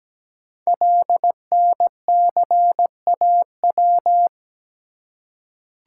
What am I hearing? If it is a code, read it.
Morse code, heard as a keyed tone, about 17 wpm, 700 Hz. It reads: LNCAW
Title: Code Groups - 5 Letters - 17wpm